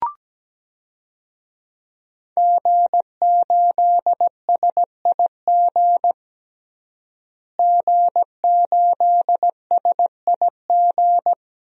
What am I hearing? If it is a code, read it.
Morse code, heard as a keyed tone, about 17 wpm, 700 Hz. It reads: G8SIG G8SIG
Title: DXCC-100 Call Signs - 5 Characters - Repeated 3x - 17wpm